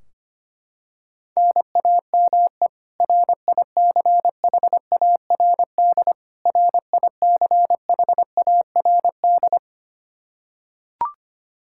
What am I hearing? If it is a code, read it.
Morse code, heard as a keyed tone, about 25 wpm, 700 Hz. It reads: NAME RICHARD RICHARD
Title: Practice QSO - Exchange 1 - Beginning - 25wpm